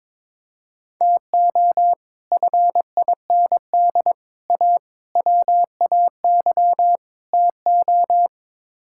Morse code, heard as a keyed tone, about 22 wpm, 700 Hz. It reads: TO FIND A WAY TO
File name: practice / 22wpm / Build A Phrase - 5-Word Phrases - 22wpm